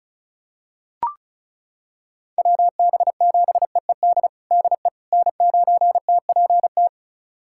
Morse code, heard as a keyed tone, about 36 wpm, 700 Hz. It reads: WB7EED DE N9TPT
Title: USA Callsign DE Callsign 35wpm